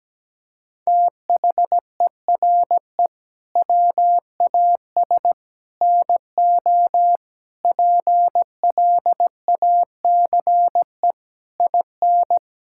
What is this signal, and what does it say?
Morse code, heard as a keyed tone, about 17 wpm, 700 Hz. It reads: THERE WAS NO PLACE IN
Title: Sentences from Top 1600 Words 17wpm